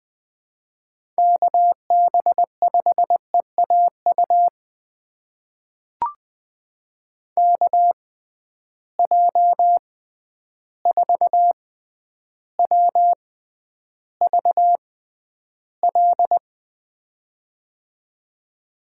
Morse code, heard as a keyed tone, about 20 wpm, spaced as for 10 wpm, 700 Hz. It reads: KB5EAU KJ4WVL
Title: USA Call Signs - 6 Characters - 6x Character Spacing - 20wpm